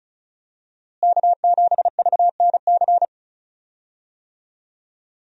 Morse code, heard as a keyed tone, about 36 wpm, 700 Hz. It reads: K7VNC
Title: USA Call Signs - 5 Characters - Repeated 3x - 35wpm